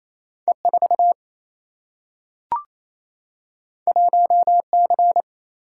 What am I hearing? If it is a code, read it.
Morse code, heard as a keyed tone, about 28 wpm, 700 Hz. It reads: E 4 1 C
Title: Single-Letter Single-Number 28wpm